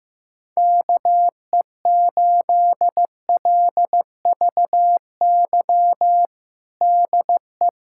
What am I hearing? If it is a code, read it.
Morse code, heard as a keyed tone, about 15 wpm, 700 Hz. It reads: KE8LVY DE
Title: USA Callsign DE Callsign - 6 Characters - 15wpm